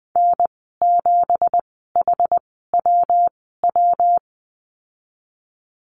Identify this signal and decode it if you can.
Morse code, heard as a keyed tone, about 19 wpm, spaced as for 14 wpm, 700 Hz. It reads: N7HWW
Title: 179 Lesson 29g - Slash - Call Signs 20wpm